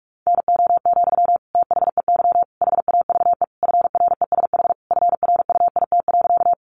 Morse code, heard as a keyed tone, about 44 wpm, 700 Hz. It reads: NO, THEY HAVE FRESH FRUIT.
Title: Sentences from Top 1300 Words 45wpm